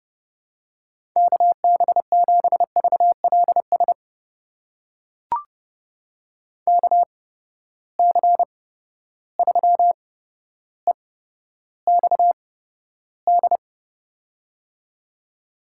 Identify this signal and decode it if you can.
Morse code, heard as a keyed tone, about 30 wpm, spaced as for 13 wpm, 700 Hz. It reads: KB7VLS KC3EXD
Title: USA Call Signs - 6 Characters - 8x Character Spacing - 30wpm